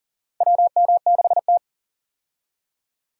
Morse code, heard as a keyed tone, about 40 wpm, 700 Hz. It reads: WM6T
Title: USA Call Signs - 4 Characters - 40wpm